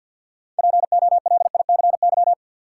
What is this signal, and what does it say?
Morse code, heard as a keyed tone, about 51 wpm, 700 Hz. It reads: POLICY
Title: Sentences from Top 1000 Words - Review Part 5 - New Word Drill - 50wpm